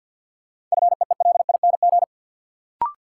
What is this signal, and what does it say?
Morse code, heard as a keyed tone, about 53 wpm, 700 Hz. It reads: FEELING
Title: Sentences from Top 1000 Words - Review Part 3 - New Word Drill - 50wpm